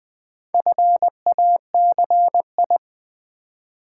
Morse code, heard as a keyed tone, about 20 wpm, 700 Hz. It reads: FACI
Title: Build A Word - 2 Letters At A Time - 8-Letter Words - 20wpm